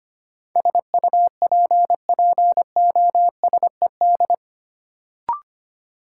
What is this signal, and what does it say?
Morse code, heard as a keyed tone, about 25 wpm, 700 Hz. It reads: SUPPOSED
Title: Sets of 3 Words - New Word Drill - 25wpm